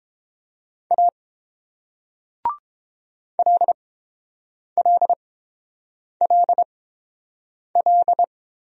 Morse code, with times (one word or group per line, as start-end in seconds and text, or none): 0.9-1.1 A
3.4-8.2 LLLL